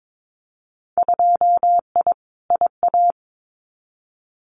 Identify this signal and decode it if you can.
Morse code, heard as a keyed tone, about 22 wpm, 700 Hz. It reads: 2I IA
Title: Winter Field Day Exchanges Encore 22wpm